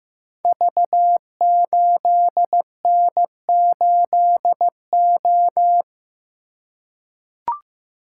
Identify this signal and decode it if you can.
Morse code, heard as a keyed tone, about 15 wpm, 700 Hz. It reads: V8N8O
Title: Code Groups - 5 Letter:Numbers - 15wpm